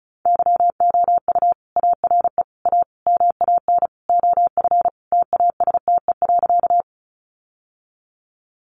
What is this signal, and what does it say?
Morse code, heard as a keyed tone, about 34 wpm, 700 Hz. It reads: YOU ARE A MAN OF TASTE.
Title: Sentences from Top 1100 Words 35wpm